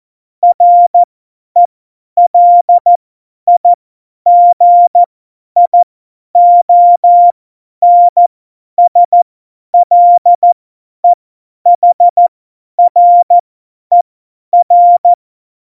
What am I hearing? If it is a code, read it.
Morse code, heard as a keyed tone, about 14 wpm, spaced as for 10 wpm, 700 Hz. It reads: RELIGIONSLEHRER
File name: Morsekurs / Witze / Witz 21. Mai 2019, 50 F70